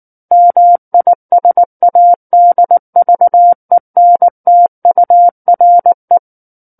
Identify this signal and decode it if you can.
Morse code, heard as a keyed tone, about 19 wpm, 700 Hz. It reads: MISADVENTURE
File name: Book One / 10-Chapter 10 - 19 WPM